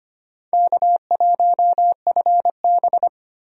Morse code, heard as a keyed tone, about 25 wpm, 700 Hz. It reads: K1FB
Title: USA Call Signs - 4 Characters - 25wpm